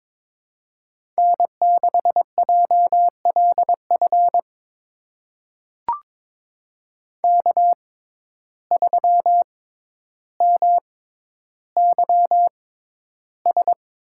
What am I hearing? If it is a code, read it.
Morse code, heard as a keyed tone, about 22 wpm, spaced as for 11 wpm, 700 Hz. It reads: N6JLF K3MYS
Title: USA Call Signs - 5 Characters - 6x Character Spacing - 22wpm